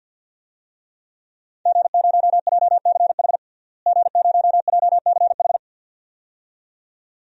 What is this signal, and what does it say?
Morse code, heard as a keyed tone, about 47 wpm, 700 Hz. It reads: G0JCH G0JCH
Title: DXCC-100 Call Signs - 5 Characters - Repeated 2x - 50wpm